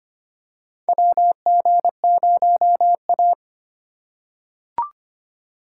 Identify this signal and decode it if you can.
Morse code, heard as a keyed tone, about 25 wpm, 700 Hz. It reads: WG0A
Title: USA Call Signs - 4 Characters - 3x Character Spacing - 25wpm